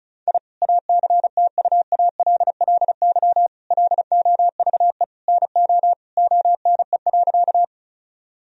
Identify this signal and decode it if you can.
Morse code, heard as a keyed tone, about 36 wpm, 700 Hz. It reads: I ACTUALLY LOVE NO ONE.
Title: Sentences from Top 1000 Words - Review Part 4 - 35wpm